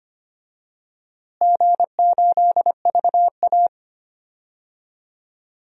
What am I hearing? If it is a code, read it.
Morse code, heard as a keyed tone, about 25 wpm, 700 Hz. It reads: G8VA
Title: DXCC-100 Call Signs - 4 Characters - Repeated 3x - 25wpm